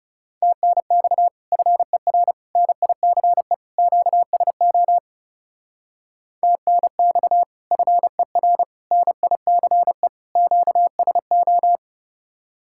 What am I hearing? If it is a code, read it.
Morse code, heard as a keyed tone, about 37 wpm, 700 Hz. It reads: TNX FER NICE QSO TNX FER NICE QSO
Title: Speed-Racing Common QSO Phrases - 30wpm